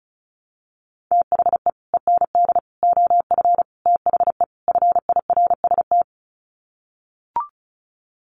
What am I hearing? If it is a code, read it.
Morse code, heard as a keyed tone, about 34 wpm, 700 Hz. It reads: THE END OF THE FIRST
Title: Sets of 5 Words Encore 35wpm